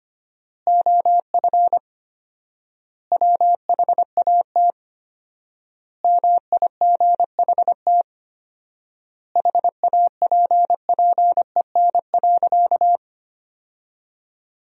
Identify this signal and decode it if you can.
Morse code, heard as a keyed tone, about 25 wpm, 700 Hz. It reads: OF WHAT MIGHT HAPPEN.
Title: 4x Word-Spacing - Sentences from Top 400 Words - 25wpm